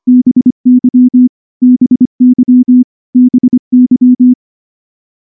Morse code, heard as a keyed tone, about 23 wpm, 250 Hz. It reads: BY BY BY
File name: morse-training-25WPM